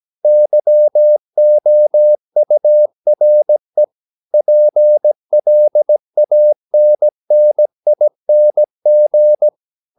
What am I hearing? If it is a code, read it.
Morse code, heard as a keyed tone, about 17 wpm, 600 Hz. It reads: YOURE PLANNING
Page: Romanen Never af Ken Follet på engelsk.